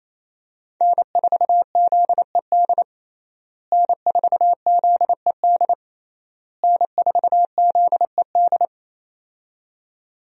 Morse code, heard as a keyed tone, about 28 wpm, 700 Hz. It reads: N4ZED N4ZED N4ZED
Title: USA Call Signs - 5 Characters - Repeated 3x - 28wpm